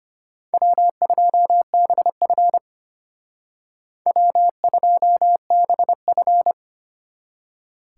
Morse code, heard as a keyed tone, about 32 wpm, 700 Hz. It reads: W2BF W2BF